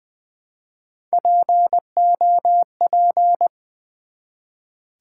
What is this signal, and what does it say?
Morse code, heard as a keyed tone, about 20 wpm, 700 Hz. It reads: POP
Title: Build A Word - 1 Letter At A Time - 3-Letter Words - 20wpm